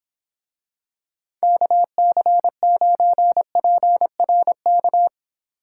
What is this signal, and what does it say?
Morse code, heard as a keyed tone, about 25 wpm, 700 Hz. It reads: KC9PRK